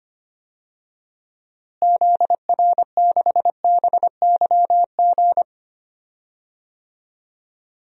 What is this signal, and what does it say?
Morse code, heard as a keyed tone, about 25 wpm, 700 Hz. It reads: ZR6BYG
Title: DXCC-100 Call Signs - 6 Characters - Repeated 2x - 25wpm